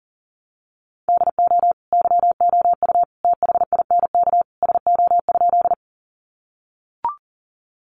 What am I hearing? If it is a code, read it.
Morse code, heard as a keyed tone, about 40 wpm, 700 Hz. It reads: DO YOU THINK SO?
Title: Sentences from Top 200 Words 40wpm